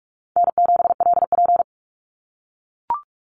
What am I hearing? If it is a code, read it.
Morse code, heard as a keyed tone, about 44 wpm, 700 Hz. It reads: N7LP
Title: NA SOTA Activator Callsigns 45wpm